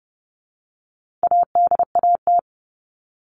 Morse code, heard as a keyed tone, about 28 wpm, 700 Hz. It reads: ADAT
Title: All Cut Numbers - 4-Digit 30wpm